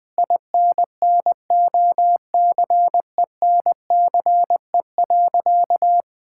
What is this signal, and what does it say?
Morse code, heard as a keyed tone, about 20 wpm, 700 Hz. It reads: INNOCENCE.
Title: Sentences with 7-Word Phrases - 20wpm